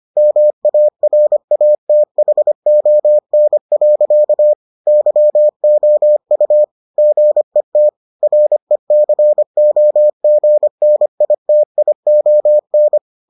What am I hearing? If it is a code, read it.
Morse code, heard as a keyed tone, about 25 wpm, 600 Hz. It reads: MARATHON. YOU GET RECOGNITION